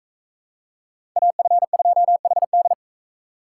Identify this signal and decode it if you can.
Morse code, heard as a keyed tone, about 44 wpm, 700 Hz. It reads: AF2HD